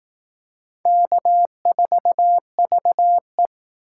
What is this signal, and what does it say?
Morse code, heard as a keyed tone, about 18 wpm, 700 Hz. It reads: K4VE
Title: Speed Racing - USA 1x2 Call Signs 15wpm